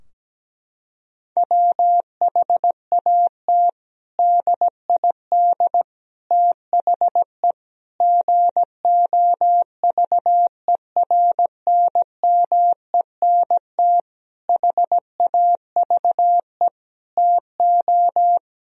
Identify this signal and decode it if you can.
Morse code, heard as a keyed tone, about 17 wpm, 700 Hz. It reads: WHAT DID THE GOVERNMENT HAVE TO
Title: Sentences from Top 1000 Words - Review Part 3 - 17wpm